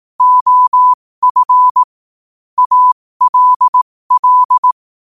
Morse code, heard as a keyed tone, about 18 wpm, spaced as for 15 wpm, 1000 Hz. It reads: OF ALL
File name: qotdfiles / 15wpm